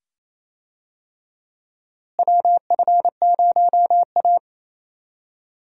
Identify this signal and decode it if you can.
Morse code, heard as a keyed tone, about 28 wpm, 700 Hz. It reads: WF0A
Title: USA Call Signs - 4 Characters - Repeated 2x - 28wpm